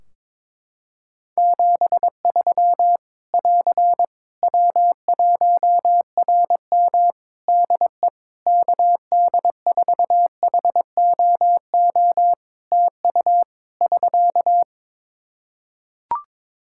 Morse code, heard as a keyed tone, about 21 wpm, 700 Hz. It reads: 73 <AR> W1RM DE KD4HOO TU <SK>
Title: Practice QSO - Exchange 4 - Ending - 22wpm